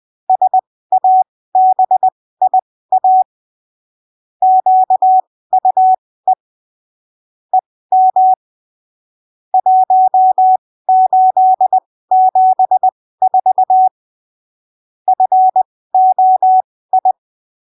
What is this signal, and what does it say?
Morse code, heard as a keyed tone, about 20 wpm, spaced as for 15 wpm, 750 Hz. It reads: SABIA QUE EM 1874 FOI
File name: arquivos / texto / texto 16 20ppm